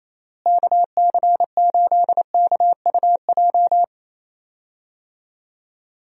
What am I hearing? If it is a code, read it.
Morse code, heard as a keyed tone, about 28 wpm, 700 Hz. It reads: KC8KUJ
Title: USA Call Signs - 6 Characters - Repeated 3x - 28wpm